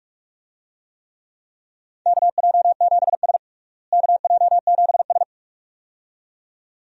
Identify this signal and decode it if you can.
Morse code, heard as a keyed tone, about 41 wpm, 700 Hz. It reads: KJ7S KJ7S
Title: USA Call Signs - 4 Characters - Repeated 2x - 45wpm